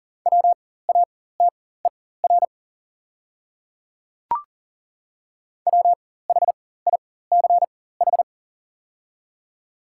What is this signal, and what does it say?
Morse code, heard as a keyed tone, about 40 wpm, spaced as for 19 wpm, 690 Hz. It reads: WATER WHICH
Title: 4x Character Spacing - Top 100 Words - 40wpm